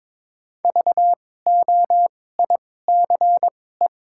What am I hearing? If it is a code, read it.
Morse code, heard as a keyed tone, about 21 wpm, spaced as for 16 wpm, 700 Hz. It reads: VOICE
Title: 2x Character Spacing - Top 500 Words - 22wpm